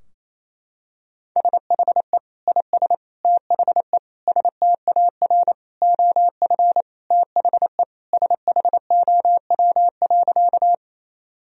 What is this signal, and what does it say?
Morse code, heard as a keyed tone, about 28 wpm, 700 Hz. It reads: SHE IS THE STAR OF THE SHOW.
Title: Sentences from Top 1000 Words - Review Part 2 - 28wpm